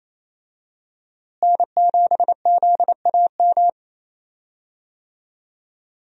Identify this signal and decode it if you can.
Morse code, heard as a keyed tone, about 28 wpm, 700 Hz. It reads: N7ZAM
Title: USA Call Signs - 5 Characters - Repeated 3x - 28wpm